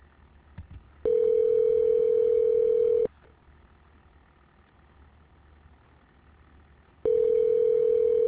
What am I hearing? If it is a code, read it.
Ringback tone (North America), 440 + 480 Hz.